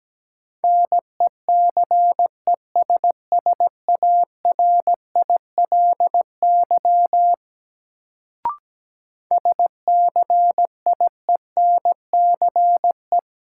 Morse code, heard as a keyed tone, about 17 wpm, 700 Hz. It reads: NECESSARILY SCIENCE
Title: Sentences from Top 1400 Words - New Word Drill Alt - 17wpm